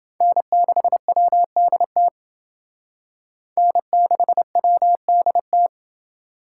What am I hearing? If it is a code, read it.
Morse code, heard as a keyed tone, about 28 wpm, 700 Hz. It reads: N6WDT N6WDT